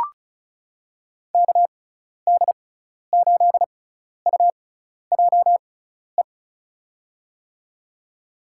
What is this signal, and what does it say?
Morse code, heard as a keyed tone, about 34 wpm, spaced as for 12 wpm, 700 Hz. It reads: KD8UJE
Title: USA Call Signs - 6 Characters - 6x Character Spacing - 35wpm